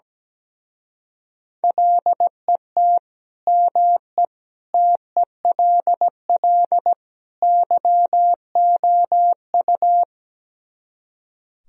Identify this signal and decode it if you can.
Morse code, heard as a keyed tone, about 17 wpm, 700 Hz. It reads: LET ME TELL YOU
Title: Speed-Racing - Sets of 4 Words 17wpm